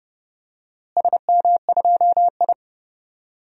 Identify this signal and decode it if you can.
Morse code, heard as a keyed tone, about 31 wpm, 700 Hz. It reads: SM2I
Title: DXCC-100 Call Signs - 4 Characters - Repeated 3x - 30wpm